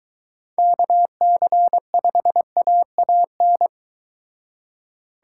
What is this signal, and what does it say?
Morse code, heard as a keyed tone, about 23 wpm, 700 Hz. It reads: KC5AAN